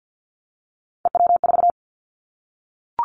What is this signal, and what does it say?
Morse code, heard as a keyed tone, about 50 wpm, 700 Hz. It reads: EW4
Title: Callsign Prefixes for Top 50 DXCC - 50wpm